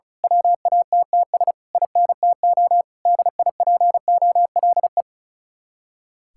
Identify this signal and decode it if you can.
Morse code, heard as a keyed tone, about 33 wpm, 700 Hz. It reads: WATTS INTO DIPOLE
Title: Speed-Racing Common QSO Phrases - 35wpm